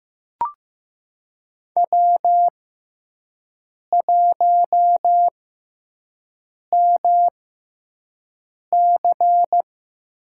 Morse code, heard as a keyed tone, about 15 wpm, spaced as for 5 wpm, 700 Hz. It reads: W1MC
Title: USA Call Signs - 5 Characters - 6x Character Spacing - 15wpm